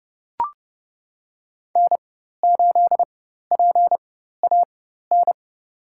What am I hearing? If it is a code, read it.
Morse code, heard as a keyed tone, about 29 wpm, spaced as for 14 wpm, 700 Hz. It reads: N8PAN
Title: USA Call Signs - 5 Characters - 4x Character Spacing - 30wpm